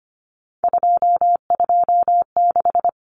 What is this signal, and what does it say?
Morse code, heard as a keyed tone, about 25 wpm, 700 Hz. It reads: 226
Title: Numbers - 3-Digit 25wpm